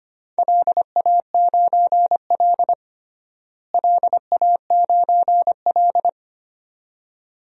Morse code, heard as a keyed tone, about 25 wpm, 700 Hz. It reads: LA9L LA9L
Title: DXCC-100 Call Signs - 4 Characters - Repeated 2x - 25wpm